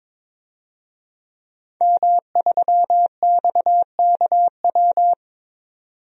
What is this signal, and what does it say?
Morse code, heard as a keyed tone, about 22 wpm, 700 Hz. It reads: M3XKW
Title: DXCC-100 Call Signs - 5 Characters - 22wpm